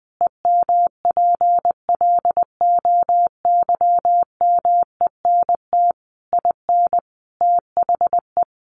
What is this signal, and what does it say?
Morse code, heard as a keyed tone, about 19 wpm, 700 Hz. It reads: EMPLOYMENT IN THE
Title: Sentences from Top 1500 Words 20wpm